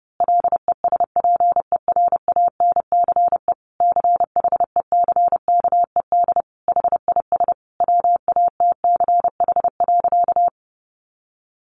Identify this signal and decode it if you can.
Morse code, heard as a keyed tone, about 30 wpm, 700 Hz. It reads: LESPERANCE CHECKED HIS WATCH.
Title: A Sound Of Thunder 30wpm